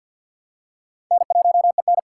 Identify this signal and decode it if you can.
Morse code, heard as a keyed tone, about 53 wpm, 700 Hz. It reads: N1EN
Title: K1USN SST QSO Practice - 50wpm